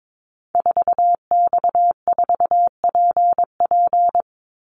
Morse code, heard as a keyed tone, about 22 wpm, 700 Hz. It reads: 4X4PP
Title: Call Signs - 22wpm